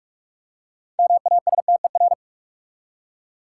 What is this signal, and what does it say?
Morse code, heard as a keyed tone, about 43 wpm, 700 Hz. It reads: MASTER
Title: Six Letter Words - 45wpm